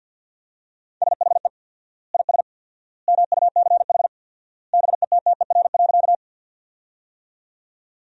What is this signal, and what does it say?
Morse code, heard as a keyed tone, about 50 wpm, 700 Hz. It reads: SHE IS MUCH BETTER.
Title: 4x Word-Spacing - Sentences from Top 400 Words - 50wpm